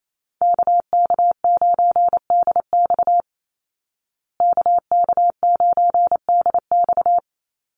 Morse code, heard as a keyed tone, about 28 wpm, 700 Hz. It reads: KK9DX KK9DX
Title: Call Signs Repeated 2x - 28wpm